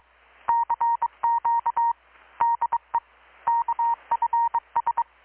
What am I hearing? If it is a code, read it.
Morse code, heard as a keyed tone, about 22 wpm, 930 Hz. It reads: CQ DE KFS
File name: blog / swl / 6-abril-2019-12695-USB-KPS